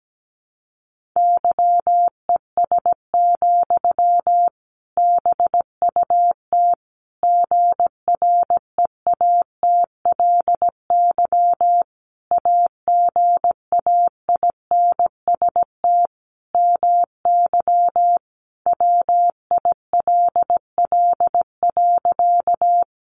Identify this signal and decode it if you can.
Morse code, heard as a keyed tone, about 17 wpm, 700 Hz. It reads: YES, BUT GREATLY AGAINST MY WILL.
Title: Sentences from Top 1400 Words 17wpm